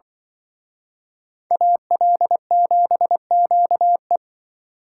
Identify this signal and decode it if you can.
Morse code, heard as a keyed tone, about 24 wpm, 700 Hz. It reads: AL7QE
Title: Speed Racing - USA 2x2 Call Signs 20wpm